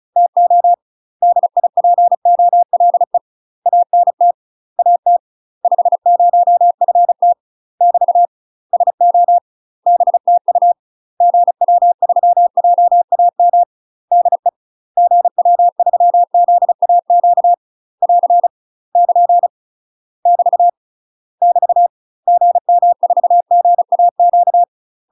Morse code, heard as a keyed tone, about 36 wpm, 700 Hz. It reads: TO DIPOLE ANT AT 50FT <BT> SO BTU GW3JAM DE GW3ZAQ <AR> <KN> <BT> <BT> GM4GAQ
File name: UK MASTER QSOs TRAINING / 35WPM / UK MASTER QSOs 0016